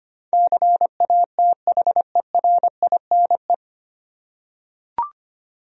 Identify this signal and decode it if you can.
Morse code, heard as a keyed tone, about 25 wpm, 700 Hz. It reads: CATHERINE
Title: USA Common Names - 25wpm